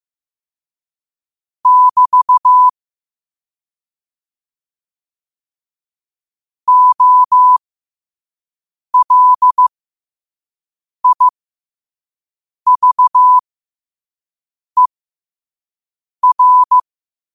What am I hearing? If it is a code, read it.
Morse code, heard as a keyed tone, about 15 wpm, spaced as for 5 wpm, 1000 Hz. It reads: <BT> OLIVER
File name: qotdfiles / 5wpm